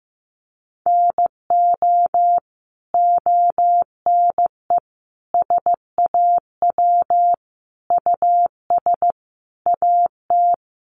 Morse code, heard as a keyed tone, about 15 wpm, 700 Hz. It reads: NO ONE SAW US AT
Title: Sentences from Top 1200 Words 15wpm